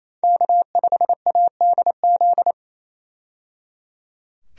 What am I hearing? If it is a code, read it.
Morse code, heard as a keyed tone, about 28 wpm, 700 Hz. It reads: K5ADZ
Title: USA Call Signs - 5 Characters - 28wpm